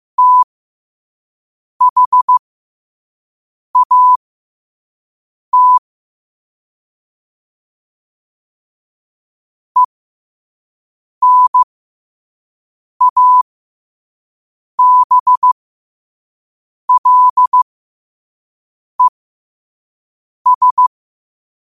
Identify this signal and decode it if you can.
Morse code, heard as a keyed tone, about 15 wpm, spaced as for 5 wpm, 1000 Hz. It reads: THAT ENABLES
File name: qotdfiles / 5wpm